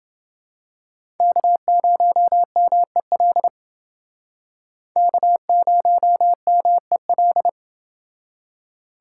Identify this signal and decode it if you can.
Morse code, heard as a keyed tone, about 28 wpm, 700 Hz. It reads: K0MEL K0MEL